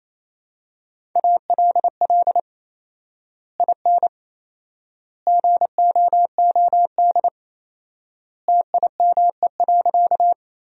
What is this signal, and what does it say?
Morse code, heard as a keyed tone, about 27 wpm, 700 Hz. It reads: ALL IN GOOD TIME.
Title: 4x Word-Spacing - Sentences from Top 200 Words - 28wpm